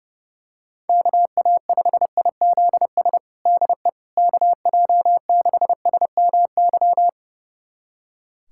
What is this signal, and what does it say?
Morse code, heard as a keyed tone, about 30 wpm, 700 Hz. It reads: KA5IZS DE KJ6SMY
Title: USA Callsign DE Callsign - 6 Characters - 30wpm